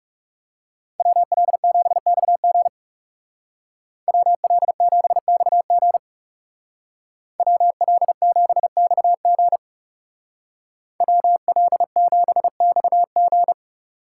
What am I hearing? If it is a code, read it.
Morse code, heard as a keyed tone, about 36 wpm, 700 Hz. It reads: WL7XG WL7XG WL7XG WL7XG
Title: Speed Racing - USA Call Signs 30wpm